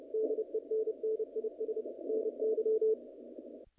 Morse code, heard as a keyed tone, about 30 wpm, 440 Hz. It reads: DENNIS NY
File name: contests / naqp-cw-k5tr-op-k5pi- -1900